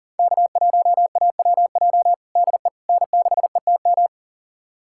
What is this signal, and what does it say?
Morse code, heard as a keyed tone, about 40 wpm, 700 Hz. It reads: K1AWJ DE N6ETM
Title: USA Callsign DE Callsign 40wpm